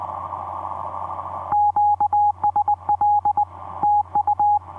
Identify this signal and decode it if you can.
Morse code, heard as a keyed tone, about 20 wpm, 840 Hz. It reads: QSL TU
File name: v5-dk1ce dm1tde etc 17m cw 28dec2010 0942